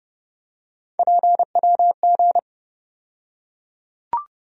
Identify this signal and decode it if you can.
Morse code, heard as a keyed tone, about 30 wpm, 700 Hz. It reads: PWG
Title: USA Call Sign Postfixes - 3 Letters 30wpm